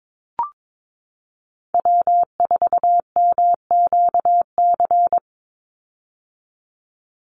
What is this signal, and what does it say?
Morse code, heard as a keyed tone, about 22 wpm, 700 Hz. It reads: W4MQC
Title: Call Signs - 22wpm